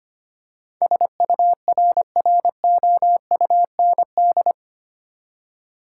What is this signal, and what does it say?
Morse code, heard as a keyed tone, about 25 wpm, 700 Hz. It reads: SURROUND
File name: Eight Letter Words - 25wpm